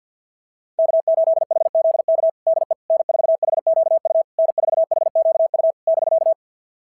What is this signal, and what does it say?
Morse code, heard as a keyed tone, about 46 wpm, 660 Hz. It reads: K8HZK DE N4HQU N4HQU <BK>
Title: SOTA QSO Practice 50wpm